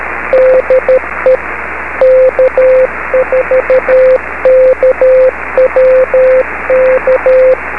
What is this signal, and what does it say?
Morse code, heard as a keyed tone, about 13 wpm, 520 Hz, with radio noise. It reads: DE K4KWK
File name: beacons / k4kwk-b